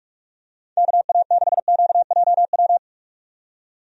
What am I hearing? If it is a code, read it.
Morse code, heard as a keyed tone, about 42 wpm, 700 Hz. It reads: KA6QJW
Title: USA Call Signs - 6 Characters - 45wpm